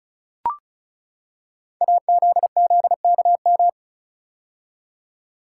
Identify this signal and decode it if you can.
Morse code, heard as a keyed tone, about 36 wpm, 690 Hz. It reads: AZZKM
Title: Code Groups - 5 Letters - 35wpm